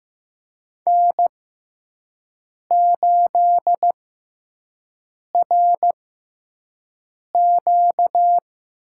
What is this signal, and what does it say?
Morse code, heard as a keyed tone, about 15 wpm, spaced as for 5 wpm, 700 Hz. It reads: N8RQ
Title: USA Call Signs - 5 Characters - 6x Character Spacing - 15wpm